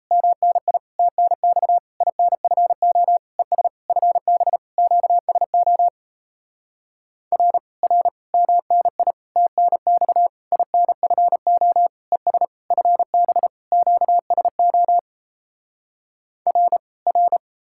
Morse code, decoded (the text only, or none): MNI TNX INFO ES FB QSO R R MNI TNX INFO ES FB QSO R R